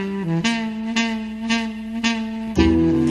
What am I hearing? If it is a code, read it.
There is no code.